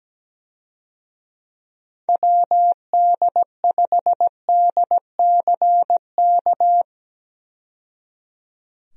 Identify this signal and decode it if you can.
Morse code, heard as a keyed tone, about 17 wpm, 700 Hz. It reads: WD5DCK
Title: USA Call Signs - 6 Characters - 17wpm